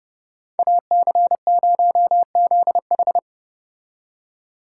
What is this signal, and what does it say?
Morse code, heard as a keyed tone, about 31 wpm, 700 Hz. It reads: AC0ZH